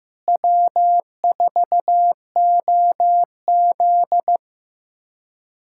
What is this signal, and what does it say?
Morse code, heard as a keyed tone, about 16 wpm, 700 Hz. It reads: W4OZ
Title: USA Callsign DE Callsign - 4 Characters - 15wpm